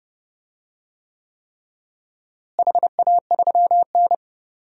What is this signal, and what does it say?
Morse code, heard as a keyed tone, about 30 wpm, 700 Hz. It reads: HA3N
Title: DXCC-100 Call Signs - 4 Characters - Repeated 3x - 30wpm